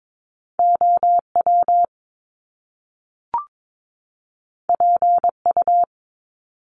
Morse code, heard as a keyed tone, about 22 wpm, 700 Hz. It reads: OW PU
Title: Top Two Letter Combinations in Words 22wpm